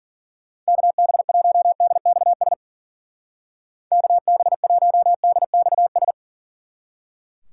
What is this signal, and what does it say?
Morse code, heard as a keyed tone, about 49 wpm, 700 Hz. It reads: KB1DXS KB1DXS
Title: Speed Racing - USA 2x3 Call Signs 40wpm